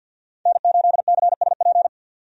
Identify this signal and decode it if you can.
Morse code, heard as a keyed tone, about 51 wpm, 700 Hz. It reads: N8CSP
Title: USA Call Signs - 5 Characters - 50wpm